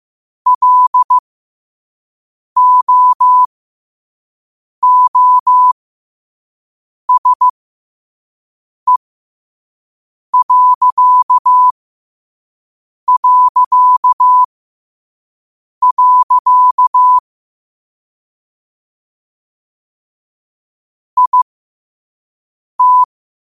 Morse code, heard as a keyed tone, about 15 wpm, spaced as for 5 wpm, 1000 Hz. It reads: LOOSE... IT
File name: qotdfiles / 5wpm